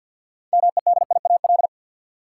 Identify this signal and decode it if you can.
Morse code, heard as a keyed tone, about 52 wpm, 700 Hz. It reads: MEDIAL
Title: Practice QSO - Exchange 3 - More Info - 50wpm